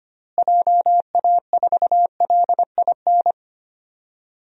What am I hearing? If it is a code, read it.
Morse code, heard as a keyed tone, about 25 wpm, 700 Hz. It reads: JA4LIN
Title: DXCC-100 Call Signs - 6 Characters - Repeated 2x - 25wpm